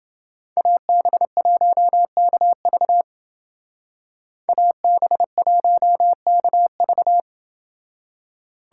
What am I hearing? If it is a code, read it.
Morse code, heard as a keyed tone, about 30 wpm, 700 Hz. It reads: AB1KV AB1KV